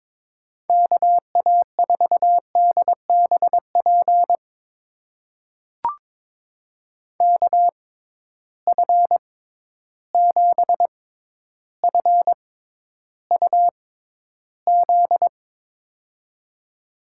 Morse code, heard as a keyed tone, about 22 wpm, spaced as for 11 wpm, 700 Hz. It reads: KA4DBP KF7FUZ
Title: USA Call Signs - 6 Characters - 6x Character Spacing - 22wpm